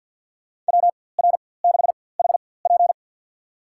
Morse code, heard as a keyed tone, about 52 wpm, spaced as for 24 wpm, 700 Hz. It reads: WR6HP
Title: USA Call Signs - 5 Characters - 4x Character Spacing - 50wpm